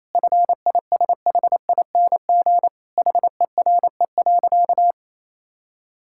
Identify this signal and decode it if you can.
Morse code, heard as a keyed tone, about 28 wpm, 700 Hz. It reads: FISHING HERE.
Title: Sentences from Top 1000 Words - Review Part 5 - 28wpm